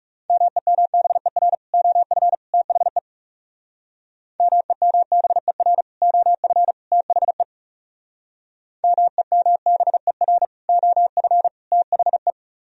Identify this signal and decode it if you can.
Morse code, heard as a keyed tone, about 37 wpm, 700 Hz. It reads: MEMBER OF THE MEMBER OF THE MEMBER OF THE
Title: Speed-Racing - Sets of 3 Words 30wpm